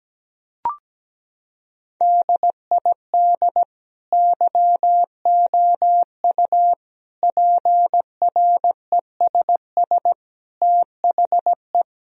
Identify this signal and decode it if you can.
Morse code, heard as a keyed tone, about 17 wpm, 700 Hz. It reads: DID YOU PRESS THE
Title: Sentences from Top 1000 Words - Review Part 1 - 17wpm